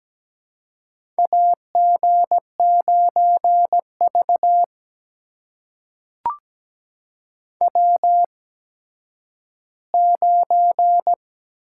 Morse code, heard as a keyed tone, about 17 wpm, spaced as for 10 wpm, 700 Hz. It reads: AG9V W9
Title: USA Call Signs - 8x Character Spacing - 17wpm